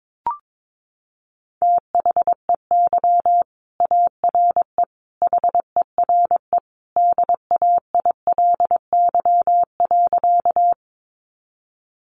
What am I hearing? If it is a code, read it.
Morse code, heard as a keyed tone, about 22 wpm, 700 Hz. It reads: THEY ARE HERE DAILY.
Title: Sentences from Top 1100 Words 22wpm